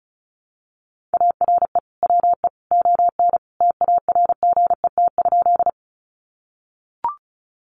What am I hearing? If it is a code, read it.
Morse code, heard as a keyed tone, about 34 wpm, 700 Hz. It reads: ARE WE ON TARGET?
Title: Sentences from Top 1200 Words 35wpm